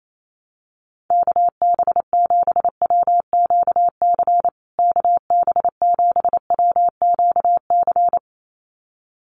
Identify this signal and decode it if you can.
Morse code, heard as a keyed tone, about 28 wpm, 700 Hz. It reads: KB7WQC KB7WQC
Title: USA Call Signs Repeated 2x 28wpm